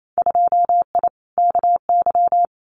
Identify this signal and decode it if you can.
Morse code, heard as a keyed tone, about 28 wpm, 700 Hz. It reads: 2I KY
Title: Winter Field Day Exchanges 28wpm